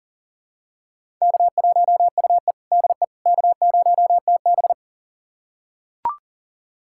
Morse code, heard as a keyed tone, about 40 wpm, 700 Hz. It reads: K1UE DE K0TB
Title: USA Callsign DE Callsign - 4 Characters - 40wpm